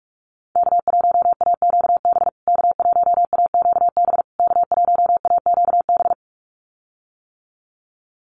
Morse code, heard as a keyed tone, about 44 wpm, 700 Hz. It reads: K1AQB K1AQB K1AQB
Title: USA Call Signs Repeated 3x 45wpm